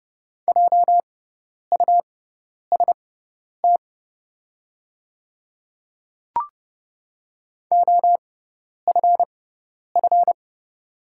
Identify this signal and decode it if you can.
Morse code, heard as a keyed tone, about 30 wpm, spaced as for 10 wpm, 690 Hz. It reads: JUST OFF
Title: 6x Character Spacing - Top 500 Words - 30wpm